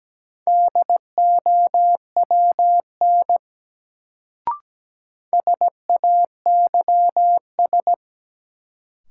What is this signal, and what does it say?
Morse code, heard as a keyed tone, about 17 wpm, 700 Hz. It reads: DOWN SAYS
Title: Sets of 2 Words - New Word Drill - 17wpm